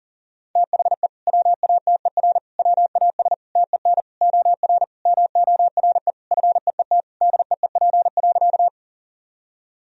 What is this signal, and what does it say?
Morse code, heard as a keyed tone, about 40 wpm, 700 Hz. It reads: THE WATER WAS TEN OR MORE FEET DEEP.
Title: Sentences from Top 1000 Words - Review Part 3 - 40wpm